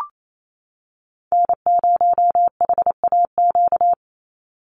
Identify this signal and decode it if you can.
Morse code, heard as a keyed tone, about 28 wpm, 700 Hz. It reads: N0HAQ
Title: USA Call Signs 28wpm